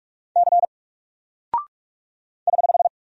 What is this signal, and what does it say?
Morse code, heard as a keyed tone, about 46 wpm, 700 Hz. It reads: C <HH>
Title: Single Character (Letter:Number:Punctuation:Prosign) Alt 45wpm